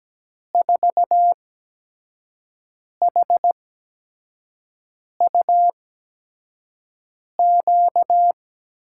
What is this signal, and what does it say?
Morse code, heard as a keyed tone, about 17 wpm, spaced as for 5 wpm, 700 Hz. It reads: 4HUQ
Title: USA Call Signs - 8x Character Spacing - 17wpm